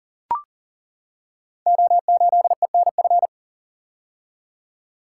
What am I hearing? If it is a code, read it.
Morse code, heard as a keyed tone, about 40 wpm, 690 Hz. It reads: O8ENF
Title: Code Groups - 5 Letter:Numbers - 40wpm